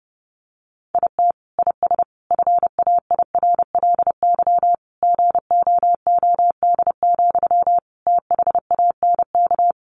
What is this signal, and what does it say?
Morse code, heard as a keyed tone, about 30 wpm, 700 Hz. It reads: IT IS FAIRLY GOOD, THANK
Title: Sentences from Top 1100 Words 30wpm